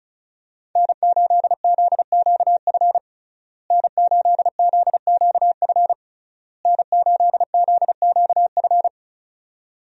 Morse code, heard as a keyed tone, about 34 wpm, 700 Hz. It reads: N8ZQF N8ZQF N8ZQF
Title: USA Call Signs - 5 Characters - Repeated 3x - 35wpm